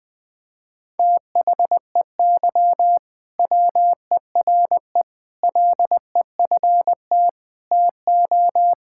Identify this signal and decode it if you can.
Morse code, heard as a keyed tone, about 20 wpm, 700 Hz. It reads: THEY WERE LEFT TO
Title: Sentences from Top 1000 Words - Review Part 5 - 20wpm